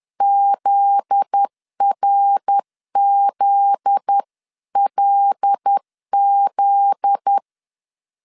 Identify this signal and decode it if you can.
Morse code, heard as a keyed tone, about 10 wpm, 800 Hz. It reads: ZRZLZ
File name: LEZIONI / 7 GF9Z / 7 GF9Z 10wpm 2611